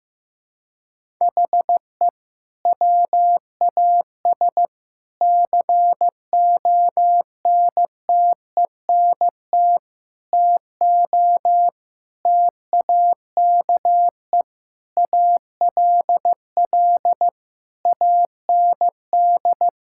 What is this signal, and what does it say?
Morse code, heard as a keyed tone, about 15 wpm, 700 Hz. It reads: HE WAS CONTENT TO TAKE ALL AND
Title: Sentences from Top 1000 Words - Review Part 6 - 15wpm